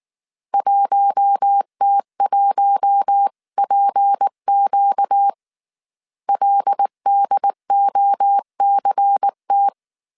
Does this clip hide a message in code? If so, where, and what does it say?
Yes: Morse code from 0.5 to 9.7 s, heard as a keyed tone, about 19 wpm, 800 Hz. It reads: 1T1PQ LDOCT